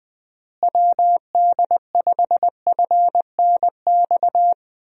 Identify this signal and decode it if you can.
Morse code, heard as a keyed tone, about 20 wpm, 700 Hz. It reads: WD5FNX
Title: USA Call Signs - 6 Characters - Repeated 2x - 20wpm